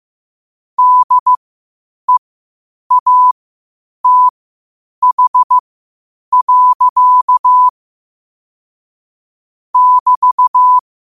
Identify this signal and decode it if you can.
Morse code, heard as a keyed tone, about 15 wpm, spaced as for 8 wpm, 1000 Hz. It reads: DEATH. <BT>
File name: qotdfiles / 8wpm